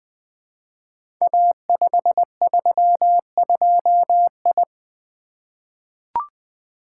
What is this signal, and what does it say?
Morse code, heard as a keyed tone, about 20 wpm, 700 Hz. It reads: A532I
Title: Code Groups - 5 Letter:Numbers - 20wpm